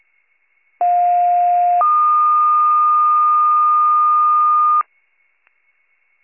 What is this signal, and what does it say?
Two-tone page, heard as a one-second tone, then a three-second tone, about 710 then 1220 Hz. It is heard from 0.8 to 4.8 s.